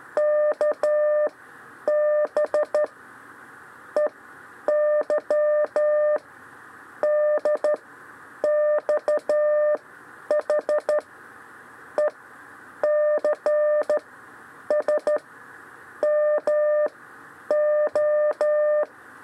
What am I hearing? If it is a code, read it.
Morse code, heard as a keyed tone, about 13 wpm, spaced as for 8 wpm, 600 Hz. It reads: KB EY DXH ECS MO
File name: 6 lezione cw 2